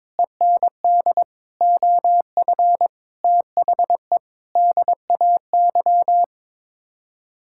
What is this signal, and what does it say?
Morse code, heard as a keyed tone, about 22 wpm, 700 Hz. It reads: END OF THE DAY
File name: practice / 22wpm / Sets of 6 Words - 22wpm